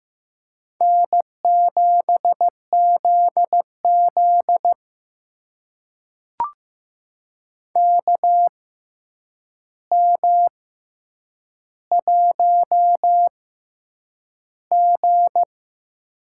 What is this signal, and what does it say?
Morse code, heard as a keyed tone, about 15 wpm, spaced as for 8 wpm, 700 Hz. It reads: N7ZZ KM1G
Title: USA Call Signs - 4 Characters - 6x Character Spacing - 15wpm